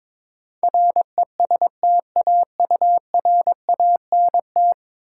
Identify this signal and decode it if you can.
Morse code, heard as a keyed tone, about 22 wpm, 700 Hz. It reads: RESTAURANT
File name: Easy Words - Speed-Racing with Hints - 22wpm